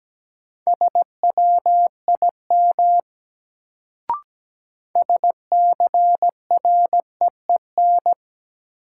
Morse code, heard as a keyed tone, about 17 wpm, 700 Hz. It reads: SWIM SCREEN
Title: Sentences from Top 1000 Words - Review Part 6 - New Word Drill - 17wpm